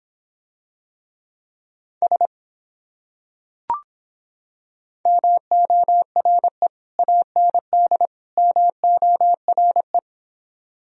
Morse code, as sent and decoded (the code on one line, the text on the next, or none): ... / -- --- .-. / .- -. -.. / -- --- .-. .
S MORE AND MORE